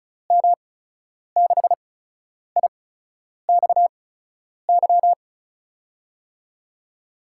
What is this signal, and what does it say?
Morse code, heard as a keyed tone, about 36 wpm, spaced as for 10 wpm, 700 Hz. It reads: M6IXY
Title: USA Call Signs - 6 Characters - 8x Character Spacing - 35wpm